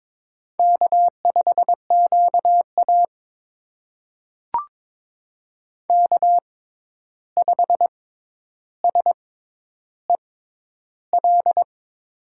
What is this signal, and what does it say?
Morse code, heard as a keyed tone, about 22 wpm, spaced as for 11 wpm, 700 Hz. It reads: K5QA K5SEL
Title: USA Call Signs - 6x Character Spacing - 22wpm